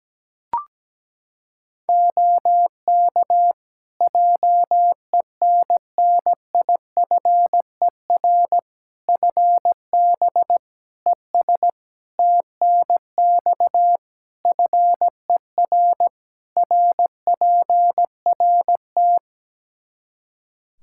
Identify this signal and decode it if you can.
Morse code, heard as a keyed tone, about 17 wpm, 700 Hz. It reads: OK JENNIFER FB ES TNX FER RPRT
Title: Practice QSO - Exchange 2 - Info - 17wpm